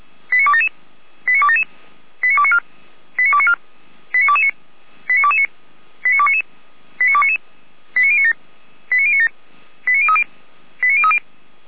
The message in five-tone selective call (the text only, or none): 89277 89277 89294 89294 89229 89229 89299 89299 89097 89097 89030 89030